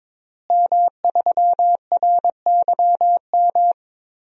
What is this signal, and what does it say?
Morse code, heard as a keyed tone, about 22 wpm, 700 Hz. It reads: M3RYM
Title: DXCC-100 Call Signs - 5 Characters - Repeated 2x - 22wpm